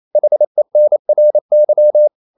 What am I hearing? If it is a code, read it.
Morse code, heard as a keyed tone, about 28 wpm, 600 Hz. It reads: HENRY
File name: CW / Practice / EnglishQuotations0228